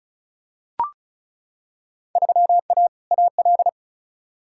Morse code, heard as a keyed tone, about 36 wpm, 700 Hz. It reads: 3A AL
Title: Field Day Exchanges - 35wpm